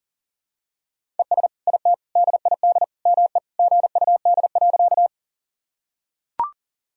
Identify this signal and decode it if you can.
Morse code, heard as a keyed tone, about 40 wpm, 700 Hz. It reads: ES IT DID ME GUD.
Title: Sentences from Top 200 Words and Abbreviations 40wpm